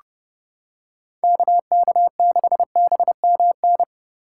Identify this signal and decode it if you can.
Morse code, heard as a keyed tone, about 30 wpm, 700 Hz. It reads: KK6BMN